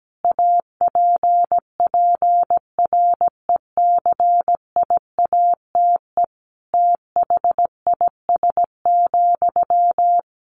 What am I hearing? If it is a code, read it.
Morse code, heard as a keyed tone, about 17 wpm, 700 Hz. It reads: APPRECIATE THIS,
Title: Sentences from Top 1300 Words 17wpm